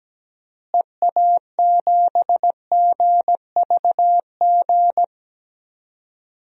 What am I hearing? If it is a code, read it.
Morse code, heard as a keyed tone, about 17 wpm, 700 Hz. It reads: EA7GVG
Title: DXCC-100 Call Signs - 6 Characters - Repeated 3x - 17wpm